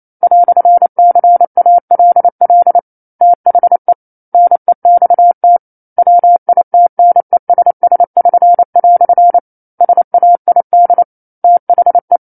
Morse code, heard as a keyed tone, about 28 wpm, 700 Hz. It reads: "CALL THE NEXT WITNESS<SN>" SAID THE